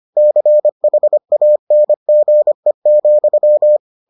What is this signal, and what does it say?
Morse code, heard as a keyed tone, about 25 wpm, 600 Hz. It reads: CHANGE,